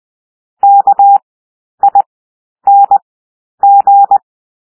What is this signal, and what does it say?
Morse code, heard as a keyed tone, about 20 wpm, 800 Hz. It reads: K I N G
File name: cwmp3 / W1 SOUND WORDS 20 wpm